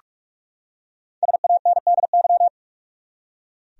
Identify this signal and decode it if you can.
Morse code, heard as a keyed tone, about 46 wpm, 700 Hz. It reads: SANDY
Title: USA Common Names - 45wpm